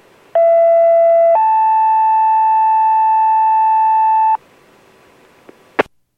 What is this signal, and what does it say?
Two-tone page, heard as a one-second tone, then a three-second tone, about 670 then 880 Hz. It is heard from 0.3 to 4.4 s.